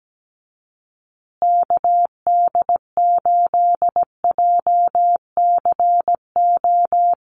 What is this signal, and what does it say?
Morse code, heard as a keyed tone, about 17 wpm, 700 Hz. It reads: KD8JCO
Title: USA Call Signs Repeated 3x 17wpm